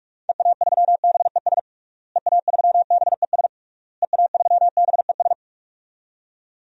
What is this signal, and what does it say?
Morse code, heard as a keyed tone, about 46 wpm, 700 Hz. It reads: EA3BES EA3BES EA3BES
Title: DXCC-100 Call Signs - 6 Characters - Repeated 3x - 45wpm